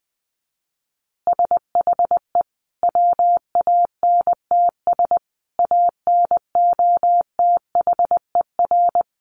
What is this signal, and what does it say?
Morse code, heard as a keyed tone, about 20 wpm, 700 Hz. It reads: SHE WANTS ANOTHER
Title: Easy Phrases 20wpm